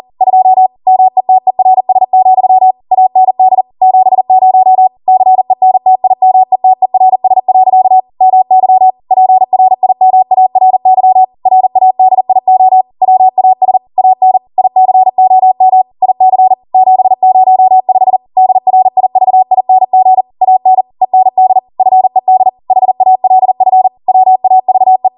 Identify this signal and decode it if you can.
Morse code, heard as a keyed tone, about 40 wpm, 750 Hz. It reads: 2 METERS, AND 70 CENTIMETERS. MY PRIMARY RADIO WAS AN ICOM IC 705 DRIVING AN END FED HALF WAVE